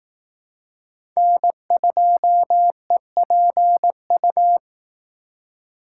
Morse code, heard as a keyed tone, about 18 wpm, 700 Hz. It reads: N2EPU
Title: Speed Racing - USA 1x3 Call Signs 15wpm